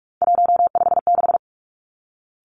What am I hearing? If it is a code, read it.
Morse code, heard as a keyed tone, about 46 wpm, 700 Hz. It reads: 156
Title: Three Challenging Letter-Number Combinations 45wpm